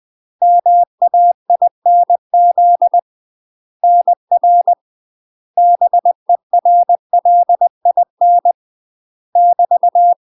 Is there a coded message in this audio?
Yes: Morse code from 0.4 to 10.1 s, heard as a keyed tone, about 20 wpm, 700 Hz. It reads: MAINZ NR BERLIN <BT>